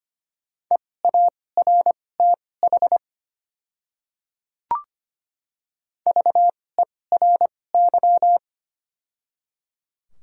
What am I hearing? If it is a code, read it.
Morse code, heard as a keyed tone, about 25 wpm, spaced as for 18 wpm, 700 Hz. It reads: EARTH VERY
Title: 2x Character Spacing - Top 200 Words - 25wpm